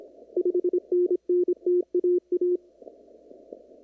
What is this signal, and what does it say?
Morse code, heard as a keyed tone, about 26 wpm, 360 Hz. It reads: 5NN TAA
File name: CW / kt5j wpx cw-op-k5pi